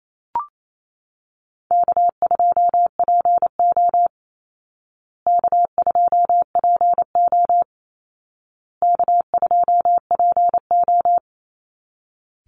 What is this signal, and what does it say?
Morse code, heard as a keyed tone, about 28 wpm, 700 Hz. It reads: K2PO K2PO K2PO
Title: Call Signs Repeated 2x - 28wpm